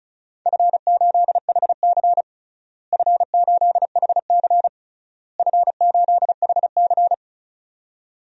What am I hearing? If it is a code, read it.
Morse code, heard as a keyed tone, about 36 wpm, 700 Hz. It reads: F8HC F8HC F8HC
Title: DXCC-100 Call Signs - 4 Characters - Repeated 3x - 35wpm